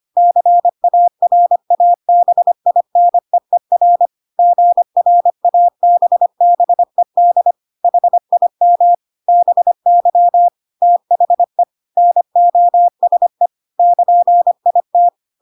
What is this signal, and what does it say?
Morse code, heard as a keyed tone, about 25 wpm, 700 Hz. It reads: CARABINEER GRABBED HIM BY THE NOSE <KN>IT
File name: Pinocchio / Pinocchio ch3 25 25 ik1hge